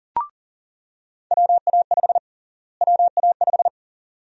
Morse code, heard as a keyed tone, about 40 wpm, 700 Hz. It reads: WA5 WA5
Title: USA Call Sign Prefixes - Repeated 2x - 40wpm